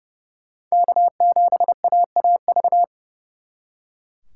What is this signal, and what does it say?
Morse code, heard as a keyed tone, about 38 wpm, 700 Hz. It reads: K7AAV